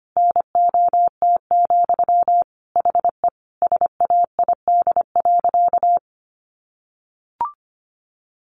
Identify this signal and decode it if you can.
Morse code, heard as a keyed tone, about 25 wpm, 700 Hz. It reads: NOT, HE SAID.
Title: Sentences from Top 1300 Words 25wpm